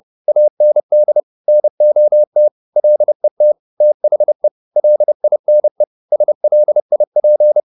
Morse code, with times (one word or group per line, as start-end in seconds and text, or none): 0.3-1.2 AND
1.5-2.5 NOT
2.8-3.5 LET
3.8-4.5 THE
4.8-5.8 LINE
6.1-7.6 SLIP